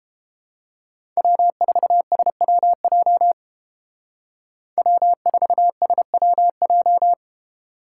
Morse code, heard as a keyed tone, about 31 wpm, 700 Hz. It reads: W4SWJ W4SWJ